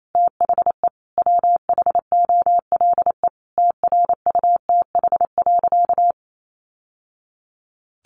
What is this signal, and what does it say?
Morse code, heard as a keyed tone, about 28 wpm, 700 Hz. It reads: THE WHOLE TRUTH.
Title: Sentences from Top 800 Words 28wpm